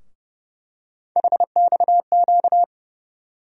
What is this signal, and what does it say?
Morse code, heard as a keyed tone, about 30 wpm, 700 Hz. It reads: HXQ
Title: USA Call Sign Postfixes - 3 Letters 30wpm